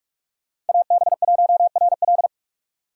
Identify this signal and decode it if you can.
Morse code, heard as a keyed tone, about 58 wpm, 700 Hz. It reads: AB1RL